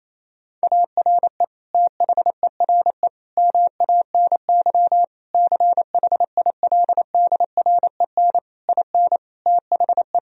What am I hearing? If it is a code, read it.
Morse code, heard as a keyed tone, about 28 wpm, 700 Hz. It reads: ARE THERE MANY CHILDREN IN THE
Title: Sentences from Top 1000 Words - Review Part 4 - 28wpm